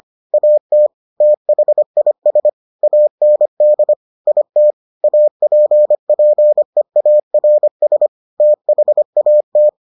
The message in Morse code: AT THIS AND IT APPEARS THAT

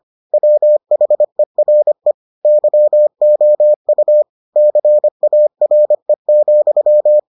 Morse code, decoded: WHERE YOU CARE,

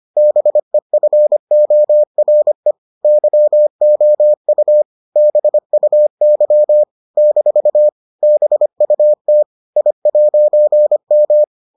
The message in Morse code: BEFORE YOU BUY - BUT I'M

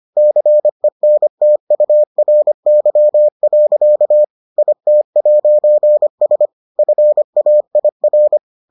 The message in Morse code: CENTURY. IT'S FAIR

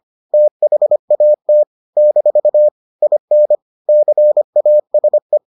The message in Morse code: THAT - IN CASE

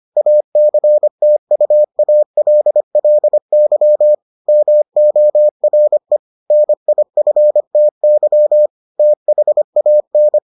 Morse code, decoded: ACTUALLY MORE NIFTY THAN

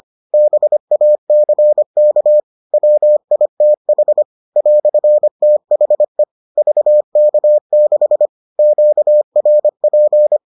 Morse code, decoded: BACK WITH "THE VK6 QRP

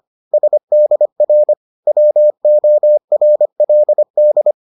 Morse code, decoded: SDR WORLD

